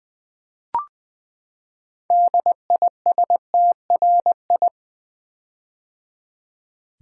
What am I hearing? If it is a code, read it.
Morse code, heard as a keyed tone, about 20 wpm, 700 Hz. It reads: DISTRI
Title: Build A Word - 6 Letters At A Time - 12-Letter Words - 20wpm